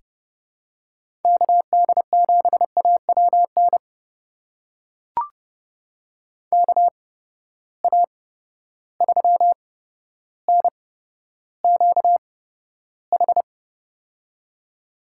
Morse code, heard as a keyed tone, about 30 wpm, spaced as for 13 wpm, 700 Hz. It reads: KD7AWN KA3NQH
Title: USA Call Signs - 6 Characters - 8x Character Spacing - 30wpm